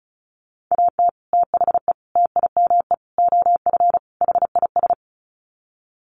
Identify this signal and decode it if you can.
Morse code, heard as a keyed tone, about 34 wpm, 700 Hz. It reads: AT THE TIME OF HIS
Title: Sets of 5 Words Encore2 35wpm